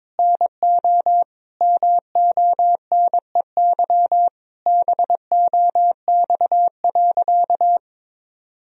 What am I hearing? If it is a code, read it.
Morse code, heard as a keyed tone, about 21 wpm, 700 Hz. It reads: NO MONEY BOX.
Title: Sentences from Top 1000 Words - Review Part 2 - 22wpm